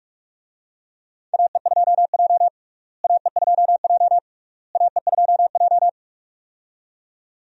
Morse code, heard as a keyed tone, about 46 wpm, 700 Hz. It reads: AE2J AE2J AE2J
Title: USA Call Signs - 4 Characters - Repeated 3x - 45wpm